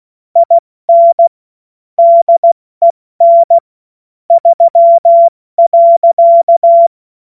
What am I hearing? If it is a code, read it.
Morse code, heard as a keyed tone, about 16 wpm, 680 Hz. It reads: IN DEN 3.
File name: Morsekurs / Witze / Witz 12 März 2024 70 F80